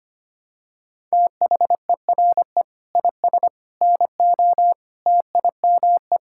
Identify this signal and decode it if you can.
Morse code, heard as a keyed tone, about 25 wpm, 700 Hz. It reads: THERE IS NO TIME
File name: practice / 25wpm / Sentences with 3-Word Phrases - 25wpm